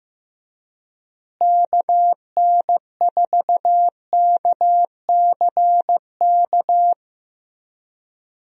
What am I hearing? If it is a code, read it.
Morse code, heard as a keyed tone, about 15 wpm, 700 Hz. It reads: KN4KCK
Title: USA Call Signs - 6 Characters - Repeated 2x - 15wpm